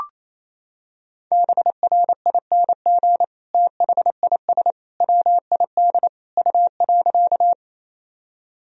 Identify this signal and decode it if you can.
Morse code, heard as a keyed tone, about 28 wpm, 700 Hz. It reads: BRING THIS WID U.
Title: Sentences from Top 500 Words and Abbreviations 28wpm